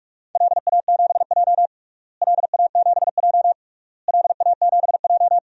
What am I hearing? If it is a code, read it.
Morse code, heard as a keyed tone, about 42 wpm, 700 Hz. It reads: LA7J LA7J LA7J
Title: DXCC-100 Call Signs - 4 Characters - Repeated 3x - 45wpm